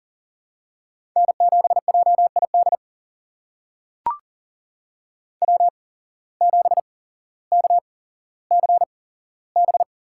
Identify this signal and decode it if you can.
Morse code, heard as a keyed tone, about 38 wpm, spaced as for 17 wpm, 700 Hz. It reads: N7JID W7KCB
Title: USA Call Signs - 5 Characters - 8x Character Spacing - 40wpm